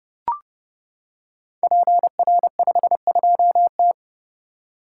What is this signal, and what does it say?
Morse code, heard as a keyed tone, about 30 wpm, 700 Hz. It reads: PR52T
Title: Code Groups - 5 Letter:Numbers - 30wpm